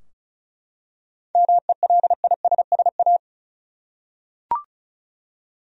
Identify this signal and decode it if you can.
Morse code, heard as a keyed tone, about 34 wpm, 700 Hz. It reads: MELISSA
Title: USA Common Names - 35wpm